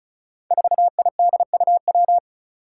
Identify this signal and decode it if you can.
Morse code, heard as a keyed tone, about 34 wpm, 700 Hz. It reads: 4IDUW
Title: Five Challenging Letter-Number Combinations 35wpm